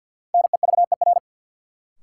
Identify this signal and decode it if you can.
Morse code, heard as a keyed tone, about 53 wpm, 700 Hz. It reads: NEVER
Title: Sentences from Top 300 Words - New Word Drill - 50wpm